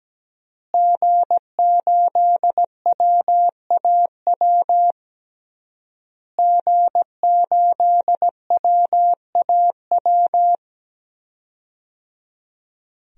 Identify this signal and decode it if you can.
Morse code, heard as a keyed tone, about 17 wpm, 700 Hz. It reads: G8WAW G8WAW
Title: DXCC-100 Call Signs - 5 Characters - Repeated 3x - 17wpm